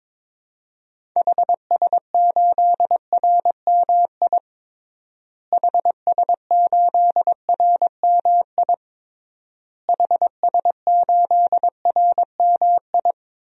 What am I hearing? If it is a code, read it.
Morse code, heard as a keyed tone, about 22 wpm, 700 Hz. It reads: HS8RMI HS8RMI HS8RMI
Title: DXCC-100 Call Signs - 6 Characters - Repeated 3x - 22wpm